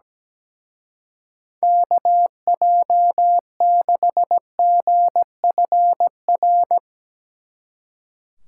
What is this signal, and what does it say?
Morse code, heard as a keyed tone, about 17 wpm, 700 Hz. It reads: KJ6GFR
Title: Speed Racing - USA 2x3 Call Signs 17wpm